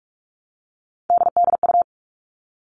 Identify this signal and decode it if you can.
Morse code, heard as a keyed tone, about 44 wpm, 700 Hz. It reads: DDU
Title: All Cut Numbers - 3-Digit 45wpm